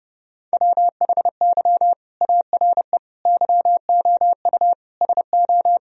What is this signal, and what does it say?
Morse code, heard as a keyed tone, about 30 wpm, 700 Hz. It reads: WHY ARE YOU SO
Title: Sentences from Top 1000 Words - Review Part 2 - 30wpm